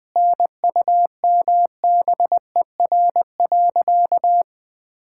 Morse code, heard as a keyed tone, about 20 wpm, 700 Hz. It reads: NUMBER.
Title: Sentences with 5-Word Phrases - 20wpm